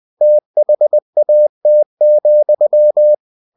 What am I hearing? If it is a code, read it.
Morse code, heard as a keyed tone, about 20 wpm, 600 Hz. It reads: THAT,
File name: cw-books / winnie-the-pooh / winnie-20wpm-page-0034